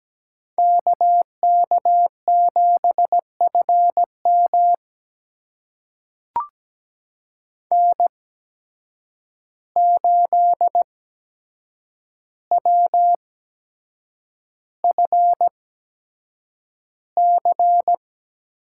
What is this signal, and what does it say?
Morse code, heard as a keyed tone, about 17 wpm, spaced as for 7 wpm, 700 Hz. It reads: KK7FM N8WFC
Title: USA Call Signs - 5 Characters - 8x Character Spacing - 17wpm